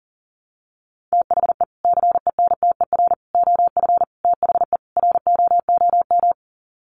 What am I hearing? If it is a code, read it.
Morse code, heard as a keyed tone, about 40 wpm, 700 Hz. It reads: THE CENTER OF THE ROOM
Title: Sets of 5 Words Encore 40wpm